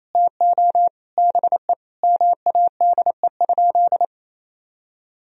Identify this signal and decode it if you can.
Morse code, heard as a keyed tone, about 28 wpm, 700 Hz. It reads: TO BE MADE?
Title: Sentences from Top 1000 Words - Review Part 4 - 28wpm